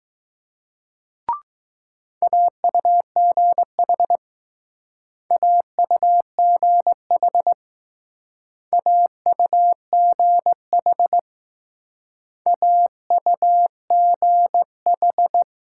Morse code, heard as a keyed tone, about 20 wpm, 700 Hz. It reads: AUGH AUGH AUGH AUGH